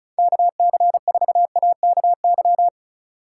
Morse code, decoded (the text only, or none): KC4AKY